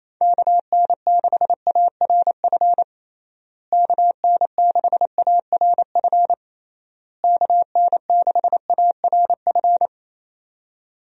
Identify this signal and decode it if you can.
Morse code, heard as a keyed tone, about 27 wpm, 700 Hz. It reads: KN6ARF KN6ARF KN6ARF
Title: USA Call Signs - 6 Characters - Repeated 3x - 28wpm